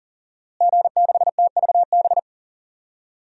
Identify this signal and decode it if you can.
Morse code, heard as a keyed tone, about 40 wpm, 700 Hz. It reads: G6TVB
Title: DXCC-100 Call Signs - 5 Characters - Repeated 3x - 40wpm